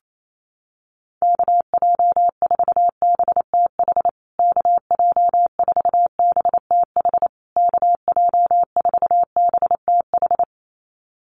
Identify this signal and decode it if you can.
Morse code, heard as a keyed tone, about 28 wpm, 690 Hz. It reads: KJ4BTH KJ4BTH KJ4BTH
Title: USA Call Signs Repeated 3x 28wpm